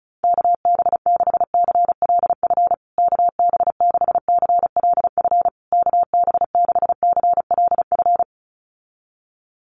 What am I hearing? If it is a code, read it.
Morse code, heard as a keyed tone, about 34 wpm, 700 Hz. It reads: KB6CLF KB6CLF KB6CLF
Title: USA Call Signs Repeated 3x 35wpm